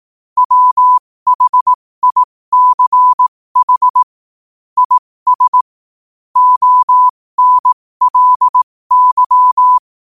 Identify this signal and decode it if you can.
Morse code, heard as a keyed tone, about 18 wpm, spaced as for 15 wpm, 1000 Hz. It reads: WHICH IS ONLY
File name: qotdfiles / 15wpm